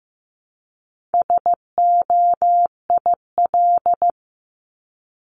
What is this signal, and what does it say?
Morse code, heard as a keyed tone, about 15 wpm, 700 Hz. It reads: SOIL
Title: Top 501-750 Words 15wpm